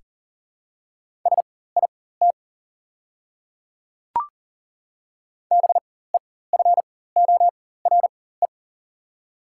Morse code, heard as a keyed tone, about 40 wpm, spaced as for 19 wpm, 690 Hz. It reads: SIT BEFORE
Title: 4x Character Spacing - Top 500 Words - 40wpm